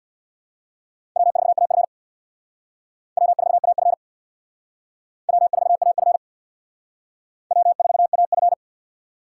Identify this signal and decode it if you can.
Morse code, heard as a keyed tone, about 61 wpm, 700 Hz. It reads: W4AF W4AF W4AF W4AF
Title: Speed Racing - USA 1x2 Call Signs 50wpm